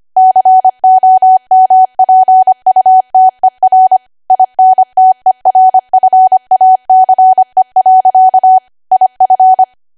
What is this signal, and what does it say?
Morse code, heard as a keyed tone, about 25 wpm, 750 Hz. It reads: COMPUTER INTERFACE. IF